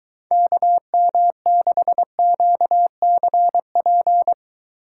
Morse code, heard as a keyed tone, about 23 wpm, 700 Hz. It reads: KM6QCP